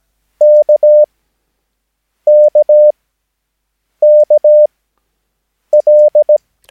Morse code, heard as a keyed tone, about 17 wpm, spaced as for 6 wpm, 600 Hz. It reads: KKKL
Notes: Övningsmaterial med alfabetet A till Ö samt siffror 0 till 9 enligt International/Continental Morse Code.